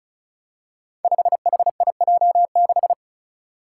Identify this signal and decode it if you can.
Morse code, heard as a keyed tone, about 35 wpm, 700 Hz. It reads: 5HIJ6
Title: Five Challenging Letter-Number Combinations 35wpm